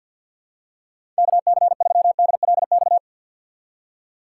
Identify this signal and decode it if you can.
Morse code, heard as a keyed tone, about 52 wpm, 700 Hz. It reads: KC3DLX
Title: USA Call Signs - 6 Characters - 4x Character Spacing - 50wpm